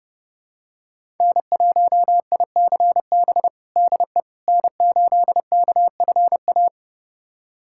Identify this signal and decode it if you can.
Morse code, heard as a keyed tone, about 32 wpm, 700 Hz. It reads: N1ICB DE N8KFA
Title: USA Callsign DE Callsign - 5 Characters - 30wpm